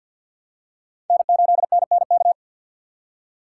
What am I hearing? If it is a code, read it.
Morse code, heard as a keyed tone, about 52 wpm, 700 Hz. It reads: N8NNK
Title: USA Call Signs - 5 Characters - 6x Character Spacing - 50wpm